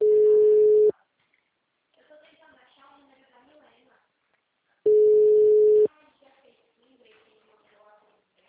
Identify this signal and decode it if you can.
Ringback tone (Europe, 425 Hz), with speech.
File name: out- -2011- - - .179